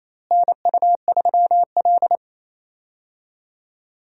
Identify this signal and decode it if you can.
Morse code, heard as a keyed tone, about 27 wpm, 700 Hz. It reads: NU3L
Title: USA Call Signs - 4 Characters - Repeated 3x - 28wpm